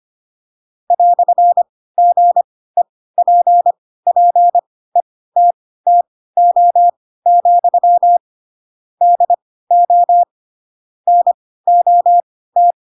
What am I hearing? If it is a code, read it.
Morse code, heard as a keyed tone, about 25 wpm, spaced as for 16 wpm, 700 Hz. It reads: "GEPPETTO, DO NOT
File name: Pinocchio / Pinocchio ch2 10 25 ik1hge